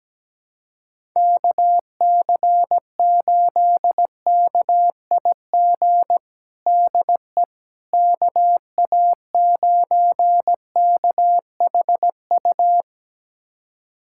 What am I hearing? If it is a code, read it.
Morse code, heard as a keyed tone, about 17 wpm, 700 Hz. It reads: KC8KIG DE KA9KHU
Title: USA Callsign DE Callsign - 6 Characters - 17wpm